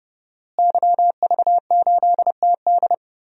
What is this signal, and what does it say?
Morse code, heard as a keyed tone, about 30 wpm, 700 Hz. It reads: YV8TD
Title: DXCC-100 Call Signs - 5 Characters - Repeated 2x - 30wpm